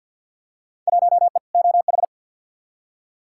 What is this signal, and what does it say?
Morse code, heard as a keyed tone, about 53 wpm, 700 Hz. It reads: 1E OH
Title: Field Day Exchanges - 50wpm